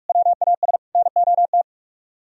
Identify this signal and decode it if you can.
Morse code, heard as a keyed tone, about 56 wpm, 700 Hz. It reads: WAS NOT